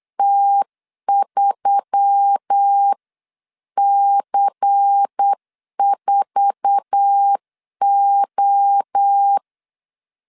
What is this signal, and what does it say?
Morse code, heard as a keyed tone, about 8 wpm, 800 Hz. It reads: T3C4O
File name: LEZIONI / 9 286 8wpm 3425